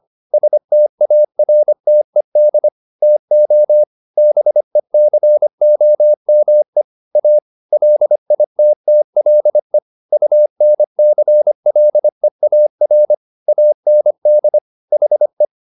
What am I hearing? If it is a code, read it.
Morse code, heard as a keyed tone, about 25 wpm, 600 Hz. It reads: STARTED TO BECOME A LITTLE UNCLEAR AND HE